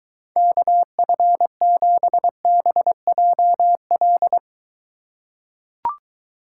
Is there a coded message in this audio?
Yes: Morse code from 0.4 to 4.4 s, heard as a keyed tone, about 23 wpm, 700 Hz. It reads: KF7BJL